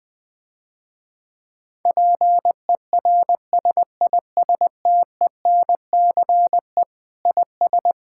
Morse code, heard as a keyed tone, about 20 wpm, 700 Hz. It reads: PERSISTENCE IS
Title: Difficult Sentences with Hints - 20wpm